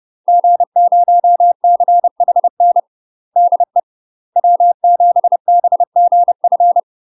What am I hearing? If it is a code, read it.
Morse code, heard as a keyed tone, about 30 wpm, 700 Hz. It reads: G0CHN DE W7BGF
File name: UK MASTER QSOs TRAINING / 30WPM / UK MASTER QSOs 0041